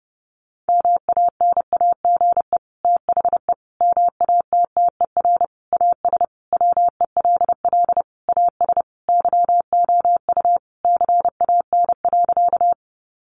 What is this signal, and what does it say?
Morse code, heard as a keyed tone, about 30 wpm, 700 Hz. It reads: MANAGE THE MATTER AS WELL AS YOU CAN.
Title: Sentences from Top 1400 Words 30wpm